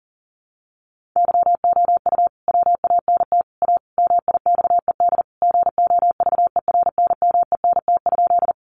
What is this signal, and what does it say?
Morse code, heard as a keyed tone, about 40 wpm, 700 Hz. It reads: YOU WANT A MIXED GOVERNMENT?
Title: Sentences from Top 1500 Words 40wpm